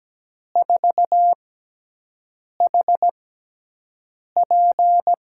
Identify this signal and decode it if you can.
Morse code, heard as a keyed tone, about 17 wpm, spaced as for 6 wpm, 700 Hz. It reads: 4HP
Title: USA Call Signs - 6 Characters - 6x Character Spacing - 17wpm